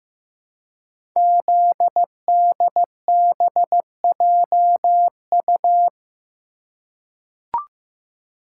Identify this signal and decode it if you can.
Morse code, heard as a keyed tone, about 15 wpm, 700 Hz. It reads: ZDBJU
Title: Code Groups - 5 Letters - 15wpm